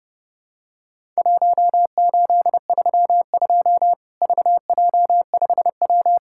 Morse code, heard as a keyed tone, about 30 wpm, 700 Hz. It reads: 1832 VJ5W
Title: Worked All Europe DX Contest - QTC Exchange - 30wpm